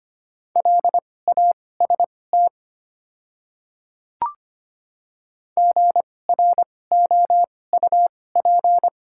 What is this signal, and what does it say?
Morse code, heard as a keyed tone, about 25 wpm, spaced as for 18 wpm, 700 Hz. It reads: LAST GROUP
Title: 2x Character Spacing - Top 300 Words - 25wpm